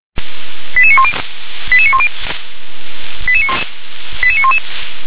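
Five-tone selective call (ZVEI), heard as quick runs of five tones, tones of about 70 ms. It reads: 88910 88910 88911
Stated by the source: ZVEI five-tone call received by SDR.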